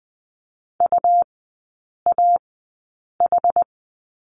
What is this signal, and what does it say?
Morse code, heard as a keyed tone, about 20 wpm, spaced as for 8 wpm, 700 Hz. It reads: UAH
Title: 121 Lesson 20e - Number 3 - Three Characters 20wpm